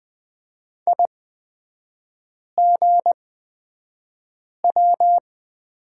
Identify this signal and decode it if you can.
Morse code, heard as a keyed tone, about 20 wpm, spaced as for 5 wpm, 700 Hz. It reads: IGW
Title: 224 Lesson 36e - BT - Three Characters 20wpm